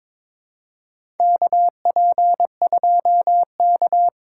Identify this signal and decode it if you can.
Morse code, heard as a keyed tone, about 22 wpm, 700 Hz. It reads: KP2K
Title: USA Call Signs - 4 Characters - 6x Character Spacing - 22wpm